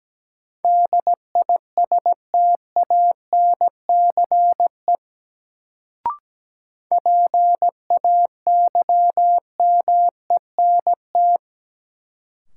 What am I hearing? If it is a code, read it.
Morse code, heard as a keyed tone, about 17 wpm, 700 Hz. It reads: DISTANCE PAYMENT
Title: Sentences from Top 1200 Words - New Word Drill Alt - 17wpm